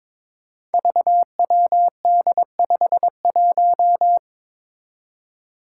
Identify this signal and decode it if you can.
Morse code, heard as a keyed tone, about 23 wpm, 700 Hz. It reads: VWD51
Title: Five Challenging Letter-Number Combinations 22wpm